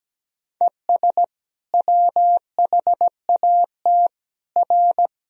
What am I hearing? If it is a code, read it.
Morse code, heard as a keyed tone, about 17 wpm, 700 Hz. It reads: ES WHAT R
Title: Sentences from Top 200 Words and Abbreviations 17wpm